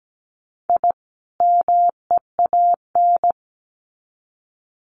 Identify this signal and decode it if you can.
Morse code, heard as a keyed tone, about 17 wpm, 700 Hz. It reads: I MEAN
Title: Sets of 2 Words 17wpm